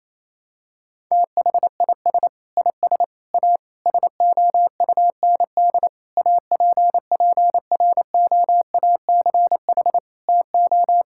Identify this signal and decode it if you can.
Morse code, heard as a keyed tone, about 27 wpm, 700 Hz. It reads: THIS IS A SOUND APPROACH TO
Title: Sentences from Top 1000 Words - Review Part 6 - 28wpm